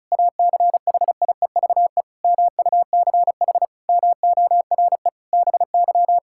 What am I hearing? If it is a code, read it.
Morse code, heard as a keyed tone, about 36 wpm, 700 Hz. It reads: ACHIEVE MUCH MORE BY
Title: Sentences from Top 1000 Words - Review Part 6 - 35wpm